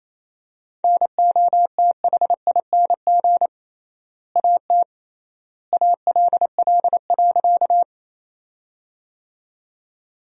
Morse code, heard as a keyed tone, about 28 wpm, 700 Hz. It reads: NOTHING AT ALL.
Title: 3x Word-Spacing - Sentences from Top 500 Words - 28wpm